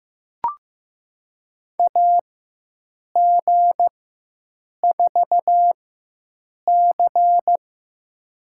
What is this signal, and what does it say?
Morse code, heard as a keyed tone, about 15 wpm, spaced as for 7 wpm, 700 Hz. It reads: AG4C
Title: USA Call Signs - 4 Characters - 4x Character Spacing - 15wpm